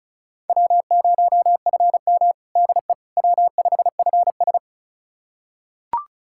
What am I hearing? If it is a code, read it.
Morse code, heard as a keyed tone, about 34 wpm, 700 Hz. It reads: W0FM DE W5FS
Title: USA Callsign DE Callsign - 4 Characters - 35wpm